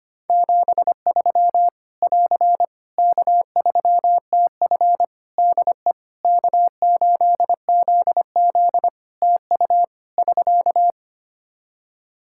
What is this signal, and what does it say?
Morse code, heard as a keyed tone, about 25 wpm, 700 Hz. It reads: 73 <AR> K3TF DE K8ZZ TU <SK>
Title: K1USN SST QSO Practice 25wpm